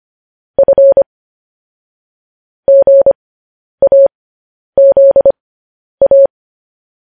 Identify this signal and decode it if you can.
Morse code, heard as a keyed tone, about 25 wpm, spaced as for 10 wpm, 550 Hz. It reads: F GAZA
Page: News Headlines in Morse Code at 10 WPM